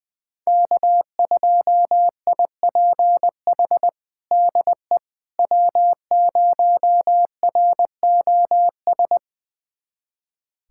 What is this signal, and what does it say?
Morse code, heard as a keyed tone, about 20 wpm, 700 Hz. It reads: K2IPH DE W0ROS
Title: USA Callsign DE Callsign - 5 Characters - 20wpm